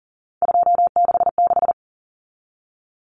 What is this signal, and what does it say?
Morse code, heard as a keyed tone, about 40 wpm, 700 Hz. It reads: 266
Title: Common Cut Numbers - 3-Digit 40wpm